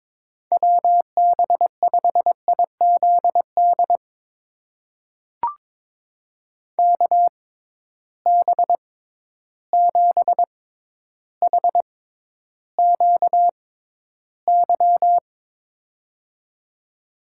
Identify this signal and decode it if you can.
Morse code, heard as a keyed tone, about 22 wpm, spaced as for 11 wpm, 700 Hz. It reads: WB5IZD KB7HQY
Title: USA Call Signs - 6 Characters - 6x Character Spacing - 22wpm